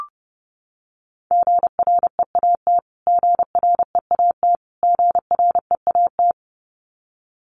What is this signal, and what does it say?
Morse code, heard as a keyed tone, about 30 wpm, 700 Hz. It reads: GREAT GREAT GREAT
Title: Top 200 Words Repeated 30wpm